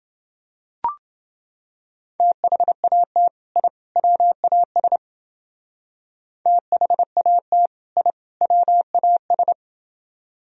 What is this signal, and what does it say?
Morse code, heard as a keyed tone, about 28 wpm, 700 Hz. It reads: THAT I WAS THAT I WAS